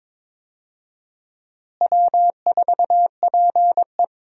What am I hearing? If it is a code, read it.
Morse code, heard as a keyed tone, about 22 wpm, 700 Hz. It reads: W4PE
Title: USA Call Signs - 4 Characters - Repeated 3x - 22wpm